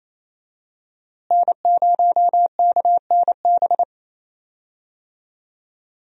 Morse code, heard as a keyed tone, about 28 wpm, 700 Hz. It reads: N0KNB
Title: USA Call Signs - 5 Characters - Repeated 2x - 28wpm